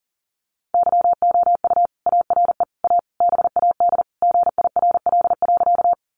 Morse code, heard as a keyed tone, about 40 wpm, 700 Hz. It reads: YOU ARE A BAD GIRL.
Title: Sentences from Top 700 Words 40wpm